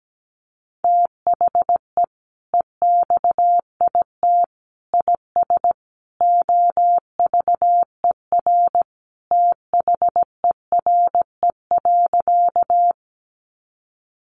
Morse code, heard as a keyed tone, about 17 wpm, 700 Hz. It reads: THE EXIT IS OVER THERE.
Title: Sentences from Top 1100 Words 17wpm